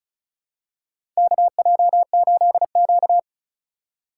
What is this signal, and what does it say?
Morse code, heard as a keyed tone, about 36 wpm, 700 Hz. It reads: KJ8Q
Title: USA Call Signs - 4 Characters - 4x Character Spacing - 35wpm